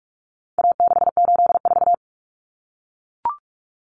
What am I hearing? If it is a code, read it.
Morse code, heard as a keyed tone, about 44 wpm, 700 Hz. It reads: A684
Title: Common Cut Numbers - 4-Digit 45wpm